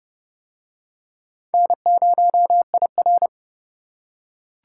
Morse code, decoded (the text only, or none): N0IR